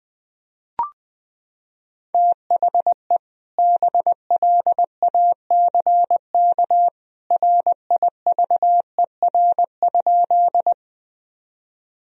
Morse code, heard as a keyed tone, about 20 wpm, 700 Hz. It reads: THE BLACK RIVER?
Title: Sentences from Top 1000 Words - Review Part 2 - 20wpm